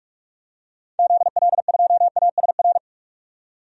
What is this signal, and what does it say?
Morse code, heard as a keyed tone, about 41 wpm, 700 Hz. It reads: ZL2ASR
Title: DXCC-100 Call Signs - 6 Characters - 45wpm